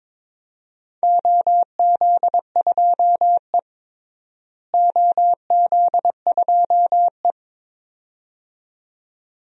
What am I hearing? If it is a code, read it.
Morse code, heard as a keyed tone, about 22 wpm, 700 Hz. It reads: OZ2E OZ2E
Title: DXCC-100 Call Signs - 4 Characters - Repeated 2x - 22wpm